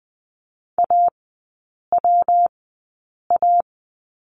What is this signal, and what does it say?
Morse code, heard as a keyed tone, about 20 wpm, spaced as for 8 wpm, 700 Hz. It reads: AWA
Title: 127 Lesson 21e - Number 6 - Three Characters 20wpm